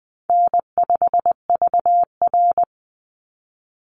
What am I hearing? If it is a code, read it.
Morse code, heard as a keyed tone, about 20 wpm, 700 Hz. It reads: N5VR
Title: Easy Call Signs - 20wpm